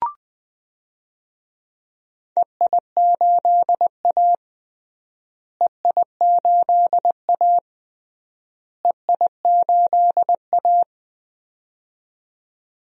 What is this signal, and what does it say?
Morse code, heard as a keyed tone, about 20 wpm, 700 Hz. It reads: EI8A EI8A EI8A
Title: DXCC-100 Call Signs - 4 Characters - Repeated 3x - 20wpm